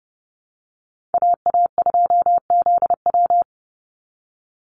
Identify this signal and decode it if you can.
Morse code, heard as a keyed tone, about 30 wpm, 700 Hz. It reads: AA2ZW
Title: Call Signs Repeated 2x - 30wpm